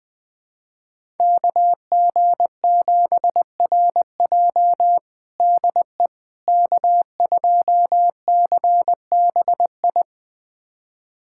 Morse code, heard as a keyed tone, about 20 wpm, 700 Hz. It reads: KG7RJ DE K2CBI
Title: USA Callsign DE Callsign - 5 Characters - 20wpm